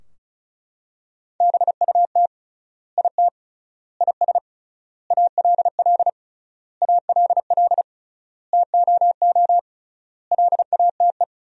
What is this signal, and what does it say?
Morse code, heard as a keyed tone, about 34 wpm, 700 Hz. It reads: BUT IT IS ALL ALL TOO LATE
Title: 3x Word-Spacing - Sentences from Top 300 Words - 35wpm